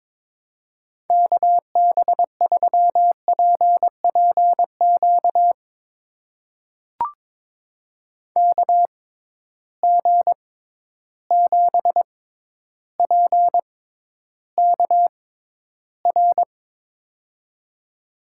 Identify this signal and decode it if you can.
Morse code, heard as a keyed tone, about 21 wpm, spaced as for 11 wpm, 700 Hz. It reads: KB3PPQ KG7PKR
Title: USA Call Signs - 6 Characters - 6x Character Spacing - 22wpm